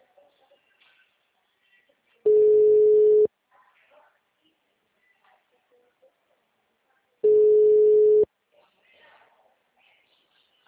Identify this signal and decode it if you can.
Ringback tone (Europe, 425 Hz), with speech.
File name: out- -2016- - - .134